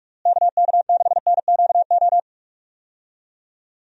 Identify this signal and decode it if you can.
Morse code, heard as a keyed tone, about 45 wpm, 700 Hz. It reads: KK6NQO
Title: USA Call Signs - 6 Characters - Repeated 2x - 45wpm